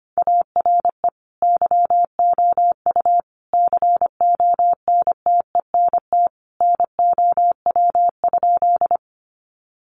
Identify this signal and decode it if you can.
Morse code, heard as a keyed tone, about 25 wpm, 700 Hz. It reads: ARE YOU CONTENT NOW?
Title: Sentences from Top 900 Words 25wpm